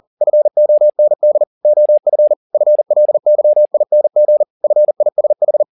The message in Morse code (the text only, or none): FOND OF FLYING FISH